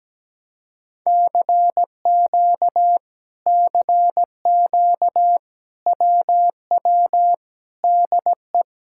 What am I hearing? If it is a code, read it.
Morse code, heard as a keyed tone, about 17 wpm, 700 Hz. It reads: CQ CQ WW DE
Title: Speed-Racing Common QSO Phrases Encore - 17wpm